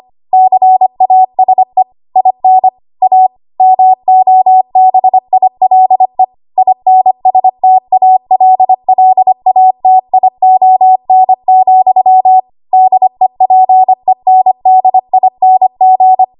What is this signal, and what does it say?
Morse code, heard as a keyed tone, about 25 wpm, 750 Hz. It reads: CASE IN A MOBILE INSTALLATION, DEPENDING
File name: Archive / 25 WPM / 25WPM